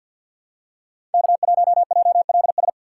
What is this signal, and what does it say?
Morse code, heard as a keyed tone, about 48 wpm, 700 Hz. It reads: K1JLS
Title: USA Call Signs - 5 Characters - 2x Character Spacing - 50wpm